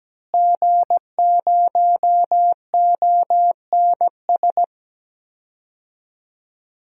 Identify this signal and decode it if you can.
Morse code, heard as a keyed tone, about 17 wpm, 700 Hz. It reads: G0ONS
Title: DXCC-100 Call Signs - 5 Characters - Repeated 3x - 17wpm